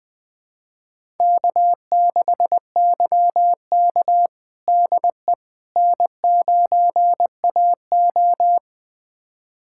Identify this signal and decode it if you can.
Morse code, heard as a keyed tone, about 20 wpm, 700 Hz. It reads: K6YK DE N9AO
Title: USA Callsign DE Callsign - 4 Characters - 20wpm